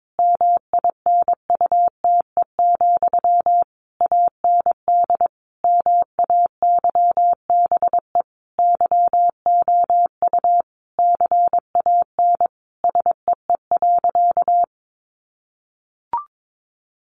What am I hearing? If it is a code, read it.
Morse code, heard as a keyed tone, about 22 wpm, 700 Hz. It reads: MINUTE, AND MAYBE YOU CAN SEE.
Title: Sentences from Top 1000 Words 22wpm